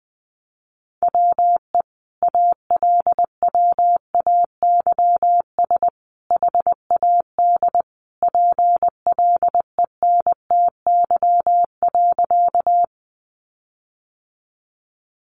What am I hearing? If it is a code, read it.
Morse code, heard as a keyed tone, about 20 wpm, 700 Hz. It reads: WE ALWAYS HAD PLENTY.
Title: Sentences from Top 1300 Words 20wpm